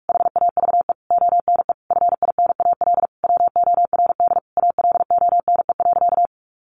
Morse code, heard as a keyed tone, about 44 wpm, 700 Hz. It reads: HAVE ONE FINAL WORD ALONE.
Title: Sentences from Top 1000 Words 45wpm